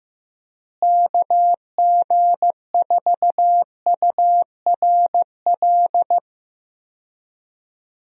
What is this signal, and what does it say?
Morse code, heard as a keyed tone, about 15 wpm, 700 Hz. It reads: KG4URL
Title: USA Call Signs - 6 Characters - Repeated 2x - 15wpm